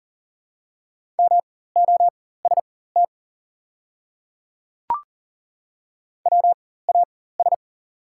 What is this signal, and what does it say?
Morse code, heard as a keyed tone, about 38 wpm, spaced as for 18 wpm, 690 Hz. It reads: MOST WAS
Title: 4x Character Spacing - Top 100 Words - 40wpm